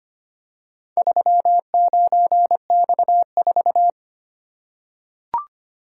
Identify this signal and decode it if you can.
Morse code, heard as a keyed tone, about 25 wpm, 700 Hz. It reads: 39X4
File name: practice / 25wpm / Four Letter-Number Combinations - 25wpm